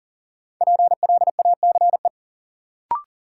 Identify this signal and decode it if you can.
Morse code, heard as a keyed tone, about 38 wpm, 700 Hz. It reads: PLACE
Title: Sets of 4 Words - New Word Drill - 40wpm